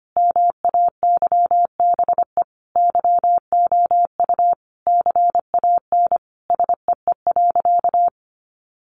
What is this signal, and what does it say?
Morse code, heard as a keyed tone, about 25 wpm, 700 Hz. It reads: MAYBE YOU CAN SEE.
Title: Sentences from Top 1000 Words 25wpm